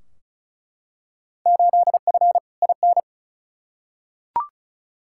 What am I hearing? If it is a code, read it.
Morse code, heard as a keyed tone, about 34 wpm, 700 Hz. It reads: 8F IN
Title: Field Day Exchanges Encore - 35wpm